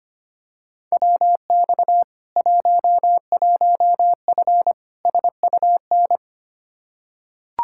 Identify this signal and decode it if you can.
Morse code, heard as a keyed tone, about 25 wpm, 700 Hz. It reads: WX 11F SUN
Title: Practice QSO - Exchange 2 - Info - 25wpm